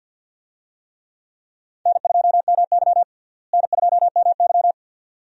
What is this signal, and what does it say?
Morse code, heard as a keyed tone, about 52 wpm, 700 Hz. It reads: N2MY N2MY
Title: USA Call Signs - 4 Characters - Repeated 2x - 50wpm